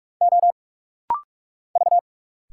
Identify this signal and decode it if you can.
Morse code, heard as a keyed tone, about 46 wpm, 700 Hz. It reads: O V
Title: Single Letter-Number - Rapid-Fire 45wpm